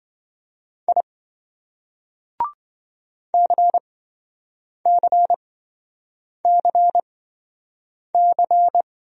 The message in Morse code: I CCCC

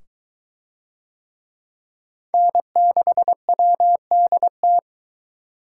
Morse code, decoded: N6WDT

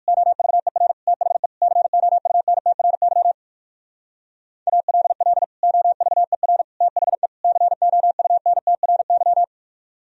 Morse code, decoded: OVER THE COUNTRY ALL OVER THE COUNTRY

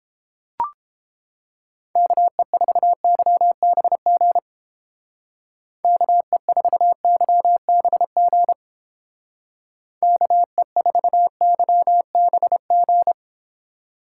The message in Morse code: KE4YBG KE4YBG KE4YBG

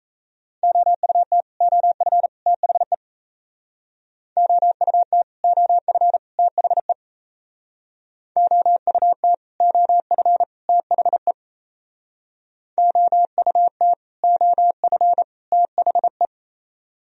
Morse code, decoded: OUT OF THE OUT OF THE OUT OF THE OUT OF THE

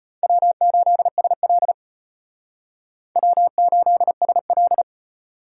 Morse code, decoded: W8SL W8SL

